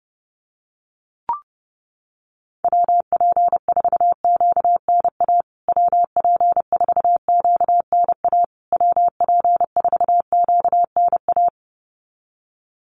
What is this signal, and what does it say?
Morse code, heard as a keyed tone, about 30 wpm, 700 Hz. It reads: WP4QNA WP4QNA WP4QNA
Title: USA Call Signs Repeated 3x 30wpm